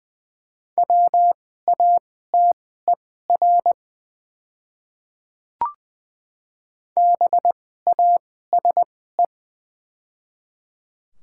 Morse code, heard as a keyed tone, about 20 wpm, spaced as for 15 wpm, 700 Hz. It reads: WATER BASE
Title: 2x Character Spacing - Top 300 Words - 20wpm